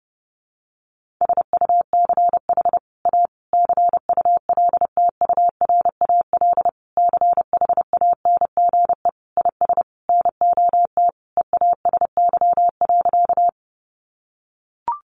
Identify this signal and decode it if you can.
Morse code, heard as a keyed tone, about 28 wpm, 700 Hz. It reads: SUCH A CULTURAL CHANGE IS NOT EASY.
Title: Sentences from Top 1500 Words 30wpm